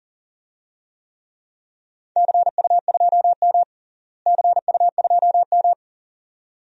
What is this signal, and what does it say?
Morse code, heard as a keyed tone, about 37 wpm, 700 Hz. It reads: CU2M CU2M
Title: DXCC-100 Call Signs - 4 Characters - Repeated 2x - 40wpm